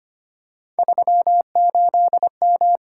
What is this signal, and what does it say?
Morse code, heard as a keyed tone, about 29 wpm, 700 Hz. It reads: 38M
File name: Speed-Racing - Three Letter-Number Combinations - 25wpm